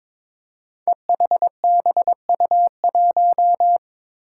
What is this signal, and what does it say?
Morse code, heard as a keyed tone, about 22 wpm, 700 Hz. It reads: EHBU1
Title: Five Challenging Letter-Number Combinations 22wpm